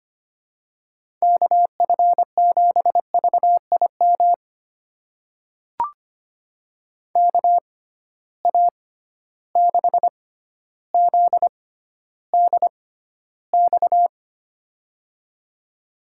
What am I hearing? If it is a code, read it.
Morse code, heard as a keyed tone, about 25 wpm, spaced as for 13 wpm, 700 Hz. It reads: KF7VIM KA6ZDX
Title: USA Call Signs - 6 Characters - 6x Character Spacing - 25wpm